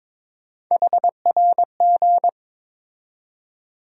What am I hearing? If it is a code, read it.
Morse code, heard as a keyed tone, about 22 wpm, 700 Hz. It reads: HRG
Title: USA Call Sign Postfixes - 3 Letters 22wpm